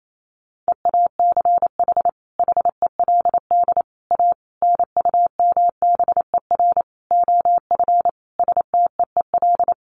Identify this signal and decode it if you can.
Morse code, heard as a keyed tone, about 28 wpm, 700 Hz. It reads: EACH HELD A NUMBER OF STEEL
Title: A Sound Of Thunder 28wpm